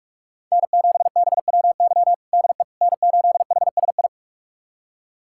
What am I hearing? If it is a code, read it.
Morse code, heard as a keyed tone, about 42 wpm, 700 Hz. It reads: N7BWY DE N8HSI
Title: USA Callsign DE Callsign - 5 Characters - 45wpm